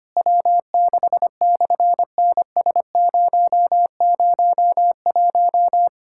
Morse code, heard as a keyed tone, about 24 wpm, 700 Hz. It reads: W6/NS001
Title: SOTA QSO Practice 25wpm